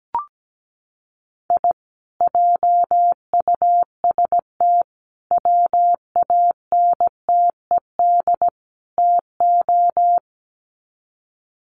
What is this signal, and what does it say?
Morse code, heard as a keyed tone, about 17 wpm, 700 Hz. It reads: I JUST WANTED TO
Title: Sets of 4 Words 17wpm